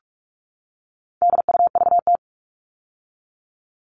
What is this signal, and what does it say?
Morse code, heard as a keyed tone, about 44 wpm, 700 Hz. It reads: DUVT
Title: All Cut Numbers - 4-Digit 45wpm